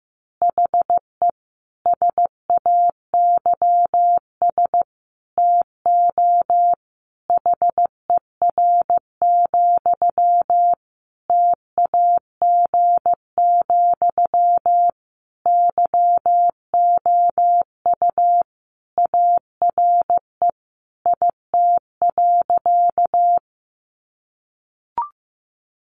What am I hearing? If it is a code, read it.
Morse code, heard as a keyed tone, about 15 wpm, 700 Hz. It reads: HE SAYS TO HER, TAG, YOU ARE IT.
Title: Additional Easy Phrases 15wpm